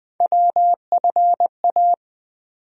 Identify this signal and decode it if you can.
Morse code, heard as a keyed tone, about 20 wpm, 700 Hz. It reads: WFA
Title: USA Call Sign Postfixes - 3 Letters 20wpm